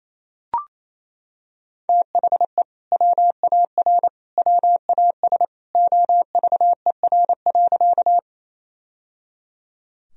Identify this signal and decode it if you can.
Morse code, heard as a keyed tone, about 28 wpm, 700 Hz. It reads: THE WAR WAS OVER.
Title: Sentences from Top 1000 Words - Review Part 2 - 28wpm